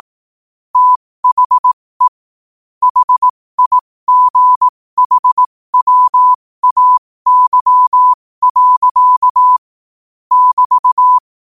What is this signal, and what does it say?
Morse code, heard as a keyed tone, about 18 wpm, spaced as for 15 wpm, 1000 Hz. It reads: THE HIGHWAY. <BT>
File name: qotdfiles / 15wpm